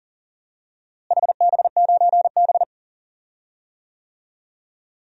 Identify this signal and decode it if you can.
Morse code, heard as a keyed tone, about 40 wpm, 700 Hz. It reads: HB9B
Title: DXCC-100 Call Signs - 4 Characters - Repeated 2x - 40wpm